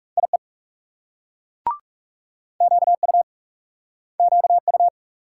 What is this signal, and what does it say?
Morse code, heard as a keyed tone, about 41 wpm, 700 Hz. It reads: IE QU QU